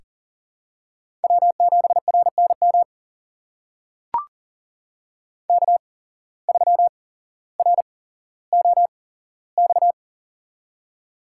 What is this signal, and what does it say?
Morse code, heard as a keyed tone, about 39 wpm, spaced as for 17 wpm, 700 Hz. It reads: W7RNM K3ROX
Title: USA Call Signs - 5 Characters - 8x Character Spacing - 40wpm